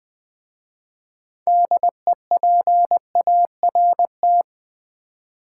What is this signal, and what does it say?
Morse code, heard as a keyed tone, about 20 wpm, 700 Hz. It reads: DEPART
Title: 6 Letter Words - First 3 Letters Given - 20wpm